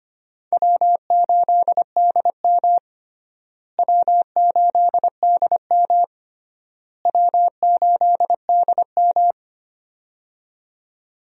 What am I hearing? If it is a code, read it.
Morse code, heard as a keyed tone, about 25 wpm, 700 Hz. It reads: W8DM W8DM W8DM
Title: USA Call Signs - 4 Characters - Repeated 3x - 25wpm